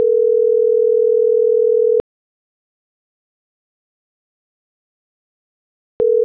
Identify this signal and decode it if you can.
Ringback tone (North America), 440 + 480 Hz.